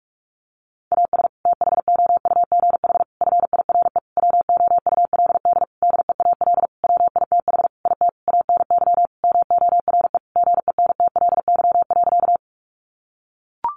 Morse code, heard as a keyed tone, about 44 wpm, 700 Hz. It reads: AS THOUGH FIRE WOULD DEAL WITH IT ANY MORE GENTLY.
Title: Sentences from Top 1500 Words 45wpm